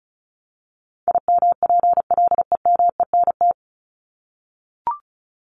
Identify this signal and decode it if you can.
Morse code, heard as a keyed tone, about 34 wpm, 700 Hz. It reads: IMPLEMENT
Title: Top 1501-1750 Words 35wpm